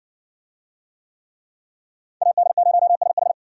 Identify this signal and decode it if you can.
Morse code, heard as a keyed tone, about 54 wpm, 700 Hz. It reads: AD0SH